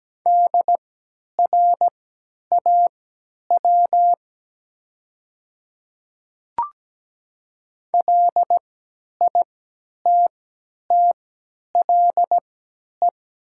Morse code, heard as a keyed tone, about 17 wpm, spaced as for 10 wpm, 700 Hz. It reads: DRAW LITTLE
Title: 3x Character Spacing - Top 500 Words - 17wpm